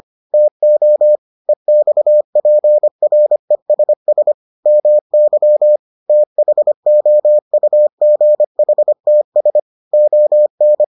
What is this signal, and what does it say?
Morse code, heard as a keyed tone, about 25 wpm, 600 Hz. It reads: TO EXPRESS MY THOUGHTS ON